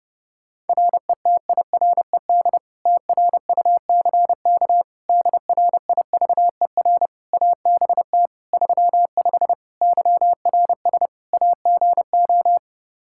Morse code, heard as a keyed tone, about 32 wpm, 700 Hz. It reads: RETIRED TRUCK DRIVER ABT 35 YRS AGO
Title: Practice QSO - Exchange 3 - More Info - 30wpm